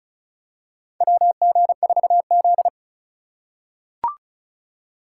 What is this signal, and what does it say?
Morse code, heard as a keyed tone, about 34 wpm, 700 Hz. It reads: WG4Z
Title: USA Call Signs - 4 Characters - 2x Character Spacing - 35wpm